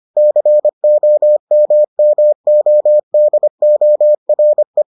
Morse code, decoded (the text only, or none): COMMODORE